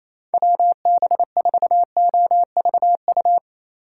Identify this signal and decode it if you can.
Morse code, heard as a keyed tone, about 34 wpm, 700 Hz. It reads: WB4OVU
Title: Speed Racing - USA 2x3 Call Signs 28wpm